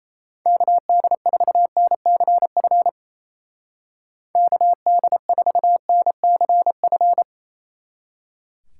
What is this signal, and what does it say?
Morse code, heard as a keyed tone, about 37 wpm, 700 Hz. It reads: KD4NCF KD4NCF